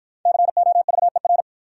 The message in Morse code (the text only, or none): COVER